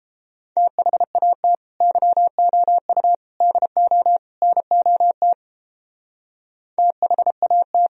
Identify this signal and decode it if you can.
Morse code, heard as a keyed tone, about 31 wpm, 700 Hz. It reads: THAT YOU DO NOT THAT